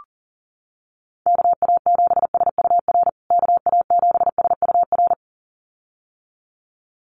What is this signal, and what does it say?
Morse code, heard as a keyed tone, about 37 wpm, 700 Hz. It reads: KA7SUR KA7SUR
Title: USA Call Signs Repeated 2x 40wpm